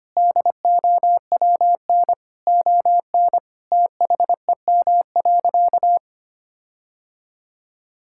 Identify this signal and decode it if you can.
Morse code, heard as a keyed tone, about 25 wpm, 700 Hz. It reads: DOWN ON THEM.
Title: Sentences from Top 1600 Words 25wpm